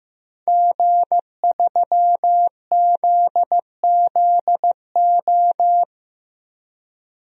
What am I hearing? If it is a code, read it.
Morse code, heard as a keyed tone, about 15 wpm, 700 Hz. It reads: G3ZZO
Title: DXCC-100 Call Signs - 5 Characters - 15wpm